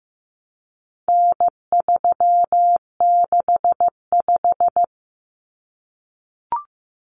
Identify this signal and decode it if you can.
Morse code, heard as a keyed tone, about 15 wpm, 700 Hz. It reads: N365
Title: Common Cut Numbers - 4-Digit 15wpm